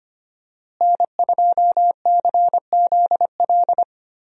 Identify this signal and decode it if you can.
Morse code, heard as a keyed tone, about 25 wpm, 700 Hz. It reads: N2CZL
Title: USA Call Signs - 5 Characters - Repeated 2x - 25wpm